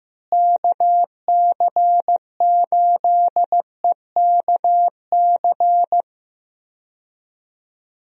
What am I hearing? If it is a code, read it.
Morse code, heard as a keyed tone, about 15 wpm, 700 Hz. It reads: KC8EKC
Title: USA Call Signs - 6 Characters - Repeated 2x - 15wpm